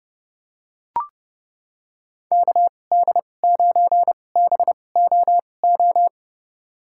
Morse code, heard as a keyed tone, about 30 wpm, spaced as for 22 wpm, 700 Hz. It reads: KD9BOO
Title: USA Call Signs - 6 Characters - 2x Character Spacing - 30wpm